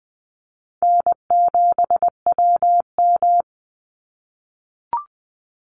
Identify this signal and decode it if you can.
Morse code, heard as a keyed tone, about 20 wpm, 700 Hz. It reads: N7WM
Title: NA SOTA Chaser Callsigns 20wpm